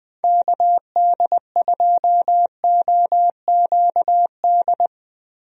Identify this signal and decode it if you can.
Morse code, heard as a keyed tone, about 20 wpm, 700 Hz. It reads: KD2OQD
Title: USA Call Signs - 6 Characters - Repeated 3x - 20wpm